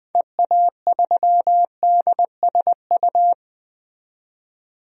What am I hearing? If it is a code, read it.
Morse code, heard as a keyed tone, about 21 wpm, 700 Hz. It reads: EA3DSU
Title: DXCC-100 Call Signs - 6 Characters - 20wpm